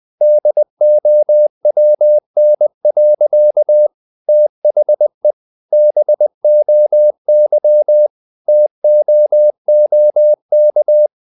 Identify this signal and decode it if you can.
Morse code, heard as a keyed tone, about 20 wpm, 600 Hz. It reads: DOWN. THE BOY TOOK
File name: the old man and the sea / the old man and the sea part 1 20wpm